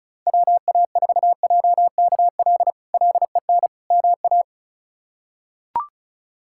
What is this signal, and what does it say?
Morse code, heard as a keyed tone, about 34 wpm, 700 Hz. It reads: WA4JKL LEN MA
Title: K1USN SST QSO Practice - 35wpm